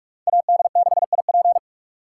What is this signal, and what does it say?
Morse code, heard as a keyed tone, about 59 wpm, spaced as for 52 wpm, 700 Hz. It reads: AD6IP
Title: Speed Racing - USA 2x2 Call Signs 45wpm